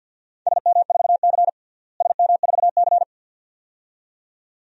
Morse code, heard as a keyed tone, about 51 wpm, 700 Hz. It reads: SM4C SM4C
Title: DXCC-100 Call Signs - 4 Characters - Repeated 2x - 50wpm